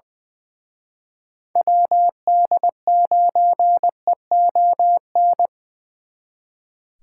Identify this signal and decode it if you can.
Morse code, heard as a keyed tone, about 24 wpm, 700 Hz. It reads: WD9EON
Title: Speed Racing - USA Call Signs 20wpm